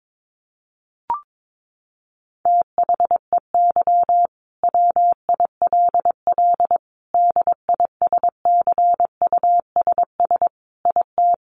Morse code, heard as a keyed tone, about 22 wpm, 700 Hz. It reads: THEY WILL DISCUSS IT
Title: Sentences from Top 1400 Words 22wpm